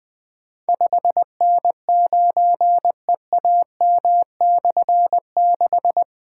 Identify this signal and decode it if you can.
Morse code, heard as a keyed tone, about 21 wpm, 700 Hz. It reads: 5N9EAM/6
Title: Difficult Call Signs 20wpm